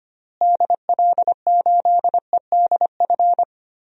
Morse code, heard as a keyed tone, about 25 wpm, 700 Hz. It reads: DL8EDF
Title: DXCC-100 Call Signs - 6 Characters - Repeated 3x - 25wpm